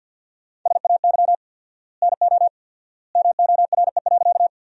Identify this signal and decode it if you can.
Morse code, heard as a keyed tone, about 50 wpm, 700 Hz. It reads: SAY NO MORE.
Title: 4x Word-Spacing - Sentences from Top 200 Words - 50wpm